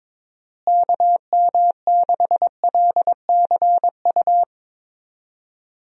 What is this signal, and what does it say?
Morse code, heard as a keyed tone, about 21 wpm, 700 Hz. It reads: KM6LCU
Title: USA Call Signs - 6 Characters - Repeated 3x - 22wpm